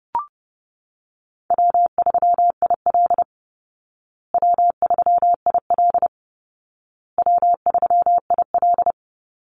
Morse code, heard as a keyed tone, about 30 wpm, 700 Hz. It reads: W3IL W3IL W3IL
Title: Easy Call Signs Repeated 3x - 30wpm